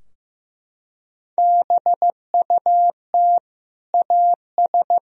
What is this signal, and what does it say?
Morse code, heard as a keyed tone, about 16 wpm, 700 Hz. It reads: BUT AS
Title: Sentences from Top 1000 Words - Review Part 3 - 15wpm